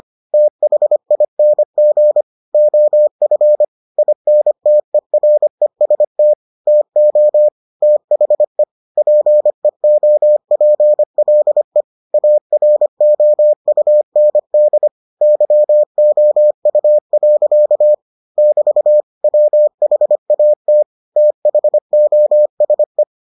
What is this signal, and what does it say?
Morse code, heard as a keyed tone, about 25 wpm, 600 Hz. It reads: THING OF INTEREST TO THE PEOPLE AROUND YOU. <BT> WHAT THOSE